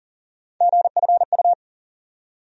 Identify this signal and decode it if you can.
Morse code, heard as a keyed tone, about 40 wpm, 700 Hz. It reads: GFU
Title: USA Call Sign Postfixes - 3 Letters 40wpm